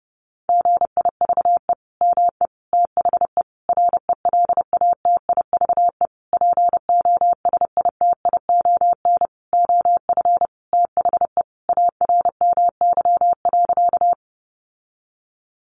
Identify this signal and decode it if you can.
Morse code, heard as a keyed tone, about 30 wpm, 700 Hz. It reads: GIVE ME THE RELATIVE POSITION OF THE ARMY.
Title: Sentences from Top 1400 Words 30wpm